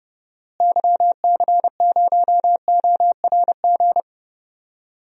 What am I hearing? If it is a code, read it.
Morse code, heard as a keyed tone, about 30 wpm, 700 Hz. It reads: YC0ORG
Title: DXCC-100 Call Signs - 6 Characters - Repeated 2x - 30wpm